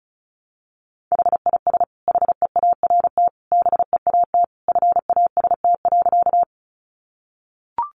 Morse code, heard as a keyed tone, about 34 wpm, 700 Hz. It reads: HIS HEART BEAT FAST.
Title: Sentences from Top 1300 Words 35wpm